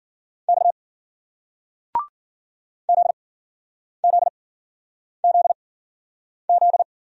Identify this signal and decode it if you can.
Morse code, heard as a keyed tone, about 51 wpm, spaced as for 9 wpm, 700 Hz. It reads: X ZZZZ